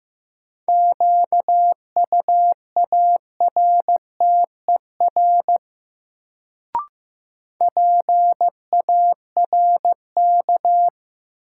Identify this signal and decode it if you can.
Morse code, heard as a keyed tone, about 15 wpm, 700 Hz. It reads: QUARTER PARK
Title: Sentences from Top 1300 Words - New Word Drill Alt - 15wpm